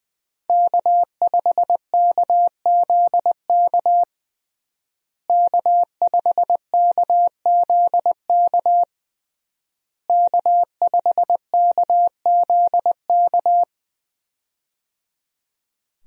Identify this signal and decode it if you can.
Morse code, heard as a keyed tone, about 20 wpm, 700 Hz. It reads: K5KZK K5KZK K5KZK
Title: USA Call Signs - 5 Characters - Repeated 3x - 20wpm